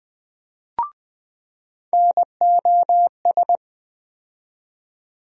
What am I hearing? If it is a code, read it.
Morse code, heard as a keyed tone, about 20 wpm, 700 Hz. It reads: NOS
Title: USA Call Sign Postfixes - 3 Letters 20wpm